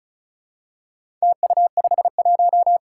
Morse code, heard as a keyed tone, about 35 wpm, 700 Hz. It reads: TU51
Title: Four Challenging Letter-Number Combinations 35wpm